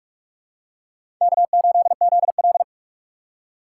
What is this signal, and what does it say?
Morse code, heard as a keyed tone, about 46 wpm, 700 Hz. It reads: K8ZL
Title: USA Call Signs - 4 Characters - 6x Character Spacing - 45wpm